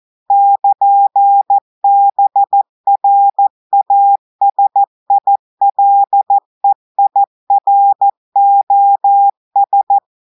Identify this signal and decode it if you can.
Morse code, heard as a keyed tone, about 14 wpm, 800 Hz. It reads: <KN>BRASILEIROS
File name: audios / portugais / portugais texte en clair 1 vitesse 14 mots minutes